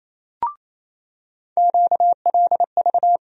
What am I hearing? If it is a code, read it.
Morse code, heard as a keyed tone, about 28 wpm, 700 Hz. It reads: QLV
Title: USA Call Sign Postfixes - 3 Letters 28wpm